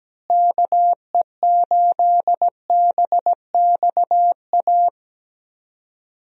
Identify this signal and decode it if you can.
Morse code, heard as a keyed tone, about 17 wpm, 700 Hz. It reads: KE8BXA
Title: USA Call Signs - 6 Characters - Repeated 3x - 17wpm